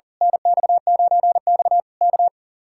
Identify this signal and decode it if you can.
Morse code, heard as a keyed tone, about 37 wpm, 700 Hz. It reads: NX9X K
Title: Speed-Racing Common QSO Phrases Encore - 40wpm